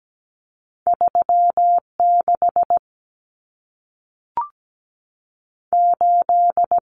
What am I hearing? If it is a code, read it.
Morse code, heard as a keyed tone, about 17 wpm, 700 Hz. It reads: 36 8
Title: Common Cut Numbers - 2-Digit 17wpm